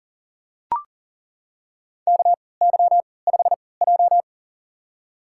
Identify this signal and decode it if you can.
Morse code, heard as a keyed tone, about 37 wpm, spaced as for 22 wpm, 700 Hz. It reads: KY5J
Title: USA Call Signs - 4 Characters - 3x Character Spacing - 40wpm